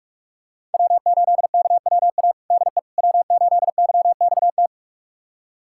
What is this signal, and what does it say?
Morse code, heard as a keyed tone, about 45 wpm, 700 Hz. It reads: W8KWA DE W8YXT
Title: USA Callsign DE Callsign - 5 Characters - 45wpm